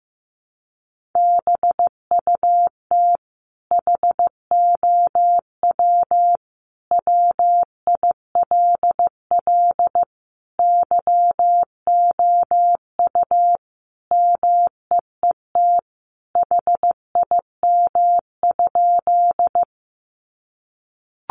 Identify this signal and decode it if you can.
Morse code, heard as a keyed tone, about 14 wpm, 700 Hz. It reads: BUT HOW WILL YOU MEET HIM?
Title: Sentences from Top 1100 Words 15wpm